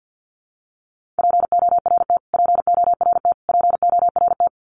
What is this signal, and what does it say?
Morse code, heard as a keyed tone, about 50 wpm, 700 Hz. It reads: PORT PORT PORT
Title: Top 300 Words Repeated 50wpm